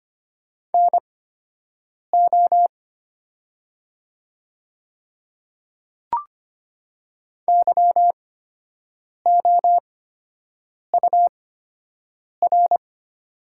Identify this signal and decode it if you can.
Morse code, heard as a keyed tone, about 25 wpm, spaced as for 7 wpm, 700 Hz. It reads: NO YOUR
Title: 8x Character Spacing - Top 100 Words - 25wpm